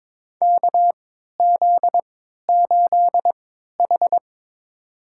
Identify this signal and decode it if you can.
Morse code, heard as a keyed tone, about 22 wpm, spaced as for 13 wpm, 700 Hz. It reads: KZ8H
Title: USA Call Signs - 4 Characters - 3x Character Spacing - 22wpm